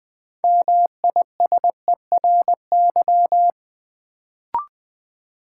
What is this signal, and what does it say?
Morse code, heard as a keyed tone, about 20 wpm, 700 Hz. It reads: MISERY
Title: Build A Word - 2 Letters At A Time - 6-Letter Words - 20wpm